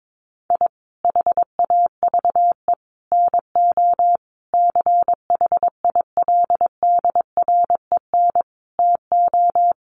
Morse code, heard as a keyed tone, about 22 wpm, 700 Hz. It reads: I HAVE NO CHILDREN TO
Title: Sentences from Top 400 Words 22wpm